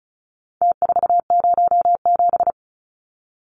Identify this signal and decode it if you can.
Morse code, heard as a keyed tone, about 34 wpm, 700 Hz. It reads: T407
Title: Numbers - Contest Style - 4-Digit 35wpm